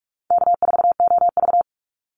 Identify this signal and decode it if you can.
Morse code, heard as a keyed tone, about 44 wpm, 700 Hz. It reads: K4OV
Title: Easy Call Signs - 45wpm